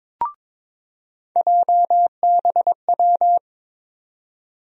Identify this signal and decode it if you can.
Morse code, heard as a keyed tone, about 22 wpm, 700 Hz. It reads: JBW
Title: USA Call Sign Postfixes - 3 Letters 22wpm